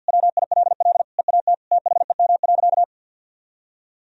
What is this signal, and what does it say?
Morse code, heard as a keyed tone, about 50 wpm, 700 Hz. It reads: WILL EAT THEM.
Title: Sentences from Top 1000 Words - Review Part 4 - 50wpm